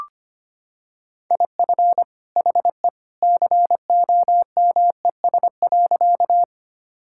Morse code, heard as a keyed tone, about 25 wpm, 690 Hz. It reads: IF HE COMES.
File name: practice / 25wpm / Sentences with 2-Word Phrases - 25wpm